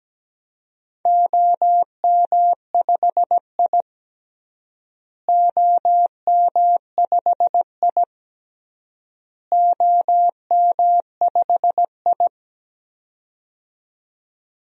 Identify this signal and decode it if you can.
Morse code, heard as a keyed tone, about 17 wpm, 700 Hz. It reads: OM5I OM5I OM5I
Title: DXCC-100 Call Signs - 4 Characters - Repeated 3x - 17wpm